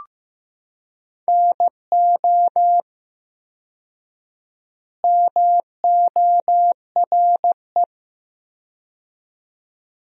Morse code, heard as a keyed tone, about 15 wpm, 700 Hz. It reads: NO MORE
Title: 4x Word-Spacing - Sentences from Top 400 Words - 15wpm